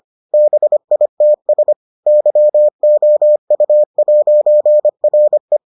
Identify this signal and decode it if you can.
Morse code, heard as a keyed tone, about 25 wpm, 600 Hz. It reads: BITS YOU'RE